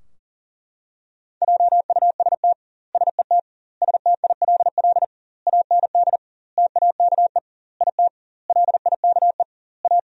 Morse code, heard as a keyed tone, about 40 wpm, 700 Hz. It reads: JUST SET STILL AND TAKE IT LIKE A
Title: 2x Word-Spacing - Sentences from Top 300 Words - 40wpm